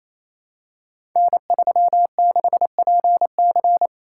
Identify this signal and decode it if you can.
Morse code, heard as a keyed tone, about 29 wpm, 700 Hz. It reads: N36PC
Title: Code Groups - 5 Letter:Numbers - 28wpm